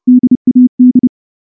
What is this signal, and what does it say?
Morse code, heard as a keyed tone, about 28 wpm, 250 Hz. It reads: DAD
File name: morse-training-30WPM